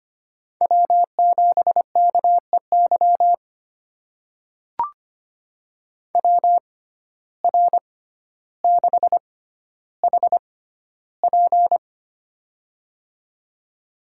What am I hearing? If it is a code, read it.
Morse code, heard as a keyed tone, about 25 wpm, spaced as for 13 wpm, 700 Hz. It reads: W7KEY WR6HP
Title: USA Call Signs - 5 Characters - 6x Character Spacing - 25wpm